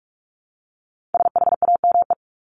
Morse code, heard as a keyed tone, about 44 wpm, 700 Hz. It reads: SHAME
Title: Top 1751-2000 Words 45wpm